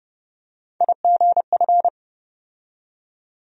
Morse code, heard as a keyed tone, about 30 wpm, 700 Hz. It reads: IGF
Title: USA Call Sign Postfixes - 3 Letters 30wpm